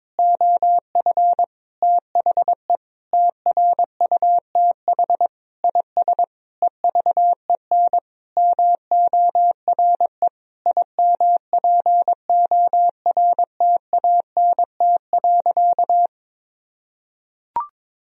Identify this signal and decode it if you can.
Morse code, heard as a keyed tone, about 22 wpm, 700 Hz. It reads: OF THE TRUTH IS EVEN MORE IMPORTANT.
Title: Sentences from Top 1000 Words - Review Part 3 - 22wpm